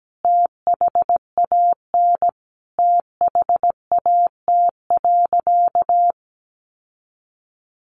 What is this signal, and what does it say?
Morse code, heard as a keyed tone, about 17 wpm, 700 Hz. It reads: THAN THAT.
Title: Sentences from Top 800 Words 17wpm